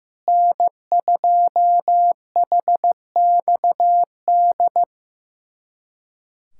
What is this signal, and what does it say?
Morse code, heard as a keyed tone, about 15 wpm, 700 Hz. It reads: N2HXD
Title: USA Callsign DE Callsign - 5 Characters - 15wpm